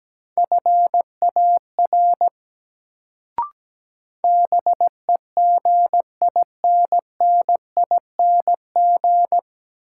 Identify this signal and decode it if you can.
Morse code, heard as a keyed tone, about 17 wpm, 700 Hz. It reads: FAR BEGINNING
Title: Sets of 3 Words - New Word Drill - 17wpm